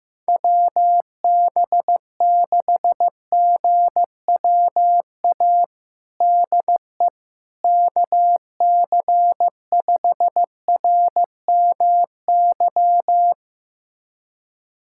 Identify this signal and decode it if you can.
Morse code, heard as a keyed tone, about 15 wpm, 700 Hz. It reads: WB6GWA DE KC5RMY
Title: USA Callsign DE Callsign 15wpm